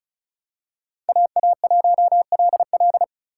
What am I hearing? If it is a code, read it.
Morse code, heard as a keyed tone, about 34 wpm, 700 Hz. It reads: AA1LL
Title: USA Call Signs - 5 Characters - 2x Character Spacing - 35wpm